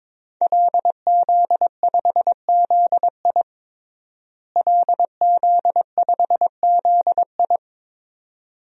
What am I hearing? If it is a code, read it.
Morse code, heard as a keyed tone, about 22 wpm, 700 Hz. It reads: LZ5ZI LZ5ZI
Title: DXCC-100 Call Signs - 5 Characters - Repeated 2x - 22wpm